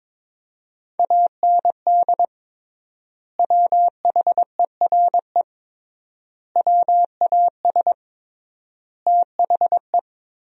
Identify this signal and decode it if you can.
Morse code, heard as a keyed tone, about 22 wpm, 700 Hz. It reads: AND WHERE WAS THE
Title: 3x Word-Spacing - Sentences from Top 200 Words - 22wpm